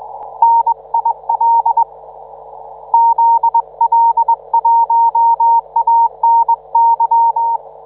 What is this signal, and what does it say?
Morse code, heard as a keyed tone, about 22 wpm, 910 Hz. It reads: NIL ZL1ANY